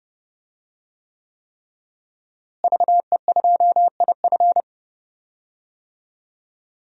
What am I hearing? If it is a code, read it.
Morse code, heard as a keyed tone, about 32 wpm, 700 Hz. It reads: VE2IF
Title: DXCC-100 Call Signs - 5 Characters - Repeated 3x - 30wpm